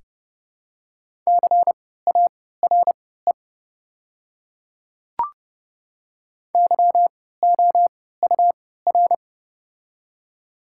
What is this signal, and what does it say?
Morse code, heard as a keyed tone, about 30 wpm, spaced as for 17 wpm, 700 Hz. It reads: CARE YOUR
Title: 3x Character Spacing - Top 300 Words - 30wpm